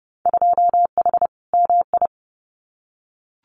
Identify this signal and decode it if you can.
Morse code, heard as a keyed tone, about 30 wpm, 700 Hz. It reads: 2H MI
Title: Winter Field Day Exchanges 30wpm